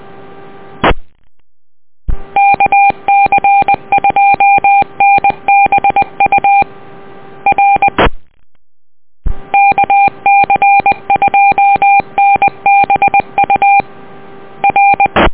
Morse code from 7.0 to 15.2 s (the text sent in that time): R KC2NBU R